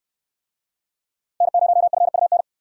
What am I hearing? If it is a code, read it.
Morse code, heard as a keyed tone, about 64 wpm, 700 Hz. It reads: N0FUN